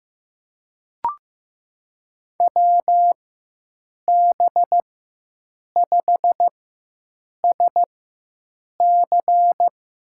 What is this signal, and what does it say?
Morse code, heard as a keyed tone, about 16 wpm, spaced as for 7 wpm, 700 Hz. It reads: WB5SC
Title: USA Call Signs - 4x Character Spacing - 15wpm